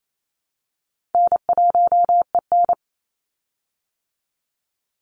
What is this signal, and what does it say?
Morse code, heard as a keyed tone, about 28 wpm, 700 Hz. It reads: N1EN
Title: Easy Call Signs Repeated 3x - 28wpm